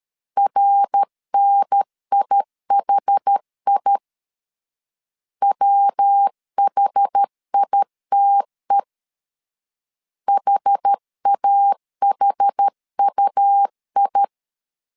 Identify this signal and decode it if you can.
Morse code, heard as a keyed tone, about 13 wpm, 800 Hz. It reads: RNIHI WHITE HAHUI
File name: CW / DET / 1 3 1500C S A 12wpm 2905